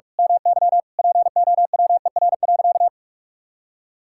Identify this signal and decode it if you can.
Morse code, heard as a keyed tone, about 44 wpm, 700 Hz. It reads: MY POWER.
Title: Sentences from Top 1000 Words - Review Part 6 - 45wpm